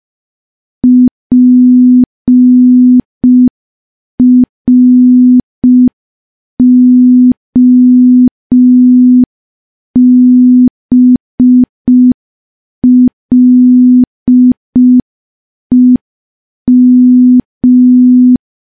Morse code, heard as a keyed tone, about 5 wpm, 250 Hz. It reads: PROBLEM
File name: morse / morse-training-05WPM